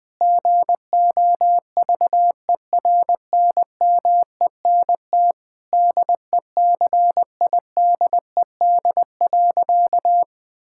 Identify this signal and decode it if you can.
Morse code, heard as a keyed tone, about 20 wpm, 700 Hz. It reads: GOVERNMENT DECIDED.
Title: Sentences with 3-Word Phrases - 20wpm